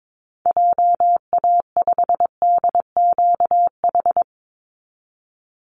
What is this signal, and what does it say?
Morse code, heard as a keyed tone, about 22 wpm, 690 Hz. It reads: JA5DQH
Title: Call Signs Repeated 3x - 22wpm